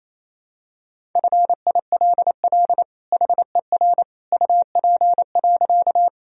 Morse code, heard as a keyed tone, about 28 wpm, 700 Hz. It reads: FILL HER UP.
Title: Sentences from Top 1000 Words - Review Part 3 - 28wpm